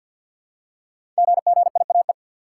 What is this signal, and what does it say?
Morse code, heard as a keyed tone, about 49 wpm, 700 Hz. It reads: GGIAE
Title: Code Groups - 5 Letters - 50wpm